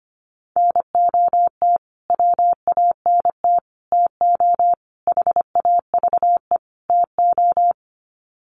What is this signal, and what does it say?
Morse code, heard as a keyed tone, about 25 wpm, 700 Hz. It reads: NOT WANT TO HAVE TO
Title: Sets of 5 Words Encore2 25wpm